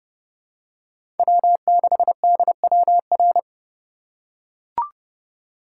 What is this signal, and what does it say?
Morse code, heard as a keyed tone, about 32 wpm, 700 Hz. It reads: W6DWR
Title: USA Call Signs - 5 Characters - 4x Character Spacing - 30wpm